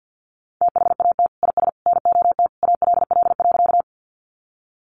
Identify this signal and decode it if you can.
Morse code, heard as a keyed tone, about 50 wpm, 700 Hz. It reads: THAT IS NOT ALL.
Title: Sentences from Top 600 Words 50wpm